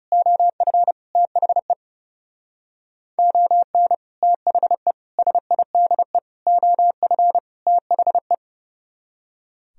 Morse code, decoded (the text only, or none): OF THE ON THE SIDE OF THE